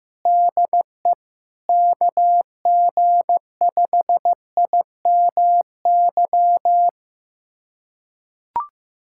Morse code, heard as a keyed tone, about 14 wpm, 700 Hz. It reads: DE KG5IMY
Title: USA Callsign DE Callsign - 6 Characters - 15wpm